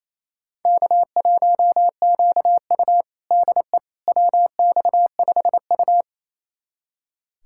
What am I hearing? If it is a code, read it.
Morse code, heard as a keyed tone, about 28 wpm, 700 Hz. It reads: K1QU DE WX5U
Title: USA Callsign DE Callsign - 4 Characters - 28wpm